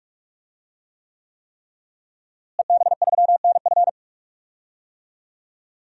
Morse code, heard as a keyed tone, about 46 wpm, 700 Hz. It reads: EB3NF
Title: DXCC-100 Call Signs - 5 Characters - Repeated 3x - 45wpm